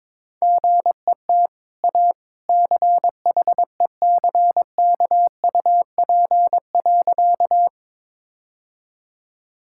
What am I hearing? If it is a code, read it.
Morse code, heard as a keyed tone, about 22 wpm, 690 Hz. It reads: GET A CHECKUP.
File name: practice / 22wpm / Easy Sentences with Hints - 22wpm